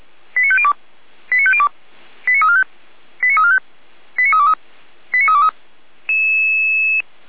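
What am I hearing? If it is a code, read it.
Five-tone selective call (ZVEI), heard as quick runs of five tones, tones of about 70 ms. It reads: 89502 89502 89345 89345 89323 89323
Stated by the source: ZVEI five-tone selective call received by SDR.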